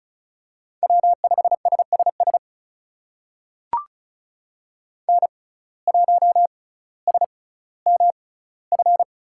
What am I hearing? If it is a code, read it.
Morse code, heard as a keyed tone, about 34 wpm, spaced as for 18 wpm, 700 Hz. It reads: W5SSS N1SMF
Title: USA Call Signs - 5 Characters - 6x Character Spacing - 35wpm